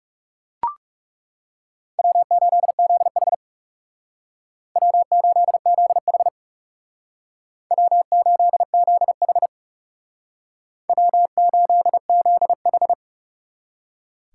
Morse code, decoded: W8ZH W8ZH W8ZH W8ZH

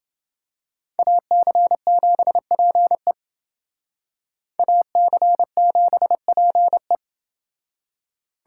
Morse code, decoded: AC7PE AC7PE

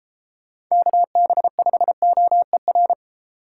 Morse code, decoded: KB5OER

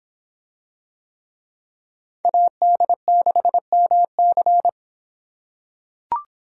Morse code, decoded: AD6MC

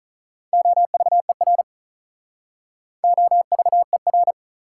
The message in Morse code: OVER OVER

